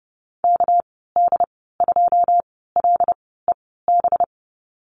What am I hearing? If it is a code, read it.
Morse code, heard as a keyed tone, about 29 wpm, spaced as for 17 wpm, 700 Hz. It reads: KD2LEB
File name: Lesson 37g - Letter x - Call Signs 30wpm